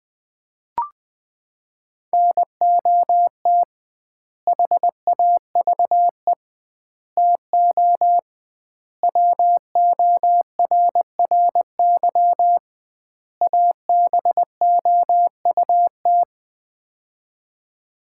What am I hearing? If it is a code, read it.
Morse code, heard as a keyed tone, about 20 wpm, 700 Hz. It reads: NOT HAVE TO WORRY ABOUT
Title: 2x Word-Spacing - Sets of 5 Words - 20wpm